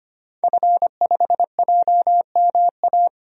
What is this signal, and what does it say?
Morse code, heard as a keyed tone, about 25 wpm, 700 Hz. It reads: F5JMA
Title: DXCC-100 Call Signs - 5 Characters - 25wpm